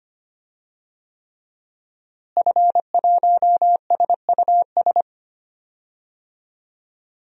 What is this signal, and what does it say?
Morse code, heard as a keyed tone, about 25 wpm, 700 Hz. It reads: F1SUS
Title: DXCC-100 Call Signs - 5 Characters - Repeated 3x - 25wpm